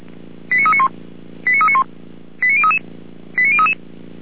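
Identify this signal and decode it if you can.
Five-tone selective call (ZVEI), heard as quick runs of five tones, tones of about 70 ms. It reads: 89391 89391 89033 89033
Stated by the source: ZVEI five-tone selective call, code 89058, received with an SDR; fire brigade Feuerwehr Orlamünde.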